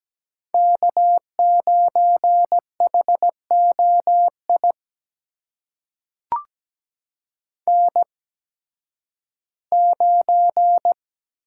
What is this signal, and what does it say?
Morse code, heard as a keyed tone, about 17 wpm, spaced as for 11 wpm, 700 Hz. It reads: K9HOI N9
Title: USA Call Signs - 8x Character Spacing - 17wpm